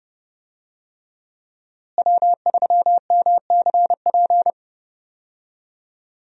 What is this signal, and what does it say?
Morse code, heard as a keyed tone, about 32 wpm, 700 Hz. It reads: W3MCP
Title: USA Call Signs - 5 Characters - Repeated 3x - 30wpm